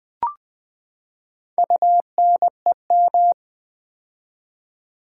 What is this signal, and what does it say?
Morse code, heard as a keyed tone, about 20 wpm, 700 Hz. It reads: UNEM
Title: Build A Word - 4 Letters At A Time - 12-Letter Words - 20wpm